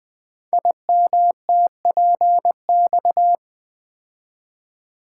Morse code, heard as a keyed tone, about 20 wpm, 700 Hz. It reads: IMTPX
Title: Code Groups - 5 Letters - 20wpm